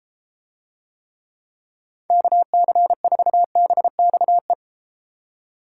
Morse code, heard as a keyed tone, about 34 wpm, 700 Hz. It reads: KC4BXE